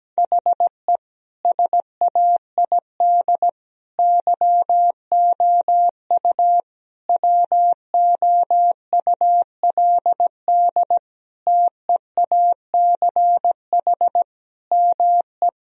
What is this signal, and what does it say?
Morse code, heard as a keyed tone, about 17 wpm, 700 Hz. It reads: HE SAID YOU WOULD TEACH ME
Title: Sentences from Top 1000 Words - Review Part 2 - 17wpm